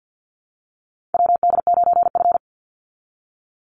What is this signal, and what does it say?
Morse code, heard as a keyed tone, about 50 wpm, 700 Hz. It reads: WD9F
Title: NA SOTA Chaser Callsigns 50wpm